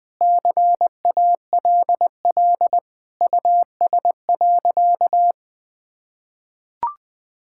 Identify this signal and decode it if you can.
Morse code, heard as a keyed tone, about 20 wpm, 700 Hz. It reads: CALL US.
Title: Sentences with 4-Word Phrases - 20wpm